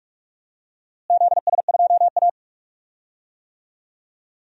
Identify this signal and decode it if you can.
Morse code, heard as a keyed tone, about 46 wpm, 700 Hz. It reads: ZS2A
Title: DXCC-100 Call Signs - 4 Characters - Repeated 2x - 45wpm